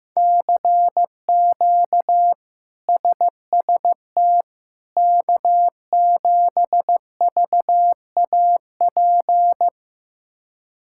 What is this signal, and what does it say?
Morse code, heard as a keyed tone, about 15 wpm, 700 Hz. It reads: CQ SST K7VAP
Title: K1USN SST QSO Practice - 15wpm